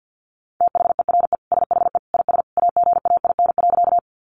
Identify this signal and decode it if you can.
Morse code, heard as a keyed tone, about 48 wpm, 700 Hz. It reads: THERE SHE IS AGAIN.
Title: Sentences from Top 200 Words 50wpm